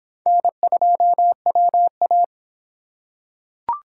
Morse code, heard as a keyed tone, about 24 wpm, 700 Hz. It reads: N2WA